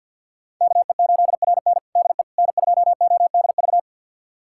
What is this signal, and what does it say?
Morse code, heard as a keyed tone, about 50 wpm, 700 Hz. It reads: KE8RN DE N2ODV
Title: USA Callsign DE Callsign - 5 Characters - 50wpm